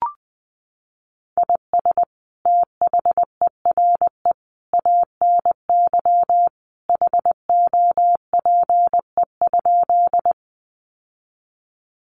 Morse code, heard as a keyed tone, about 20 wpm, 700 Hz. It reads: IS THERE ANY HOPE?
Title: Sentences from Top 1000 Words 20wpm